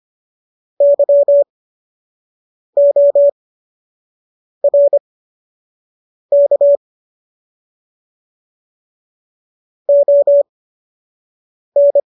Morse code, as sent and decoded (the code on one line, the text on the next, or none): -.-- --- .-. -.- / --- -.
YORK ON